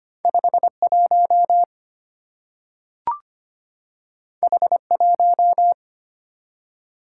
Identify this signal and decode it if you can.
Morse code, heard as a keyed tone, about 25 wpm, 700 Hz. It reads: 51 H1
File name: Two Letter-Number Combinations - 25wpm